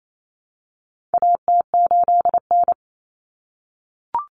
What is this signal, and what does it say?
Morse code, heard as a keyed tone, about 28 wpm, 700 Hz. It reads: AT8N
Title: Common Cut Numbers - 4-Digit 28wpm